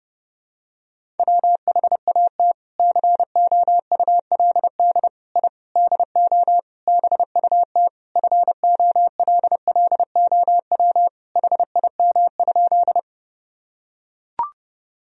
Morse code, heard as a keyed tone, about 30 wpm, 700 Hz. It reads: WHAT COULD I DO BUT FOLLOW HIM?
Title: Sentences from Top 1000 Words - Review Part 1 - 30wpm